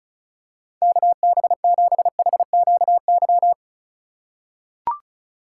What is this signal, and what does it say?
Morse code, heard as a keyed tone, about 36 wpm, 700 Hz. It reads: KB7HQY
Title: USA Call Signs - 6 Characters - 4x Character Spacing - 35wpm